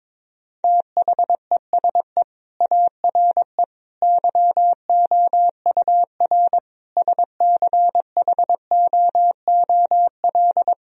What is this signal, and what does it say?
Morse code, heard as a keyed tone, about 22 wpm, 700 Hz. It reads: THESE ARE YOUR SCHOOL
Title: Sentences from Top 1000 Words - Review Part 1 - 22wpm